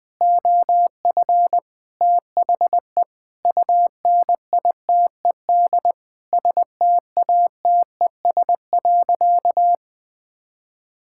Morse code, heard as a keyed tone, about 19 wpm, 700 Hz. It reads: OF THE UNITED STATES.
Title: Sentences with 8-Word Phrases - 20wpm